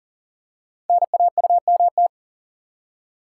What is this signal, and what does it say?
Morse code, heard as a keyed tone, about 40 wpm, 700 Hz. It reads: NAUMT
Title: Code Groups - 5 Letters - 40wpm